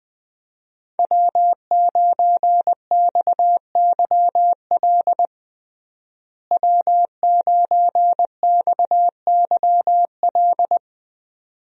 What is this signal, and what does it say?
Morse code, heard as a keyed tone, about 21 wpm, 700 Hz. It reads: W9XYL W9XYL
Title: USA Call Signs - 5 Characters - Repeated 2x - 20wpm